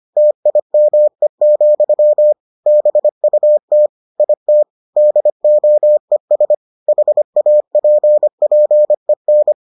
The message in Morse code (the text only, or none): TIME, BUT IT DOES HAPPEN